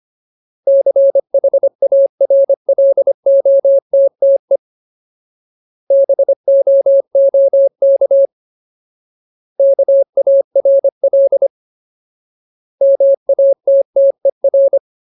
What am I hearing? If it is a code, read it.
Morse code, heard as a keyed tone, about 25 wpm, 550 Hz. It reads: CHARLOTTE BOOK KARL MATTER
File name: Ultimate Word List / 25WPM Ultimate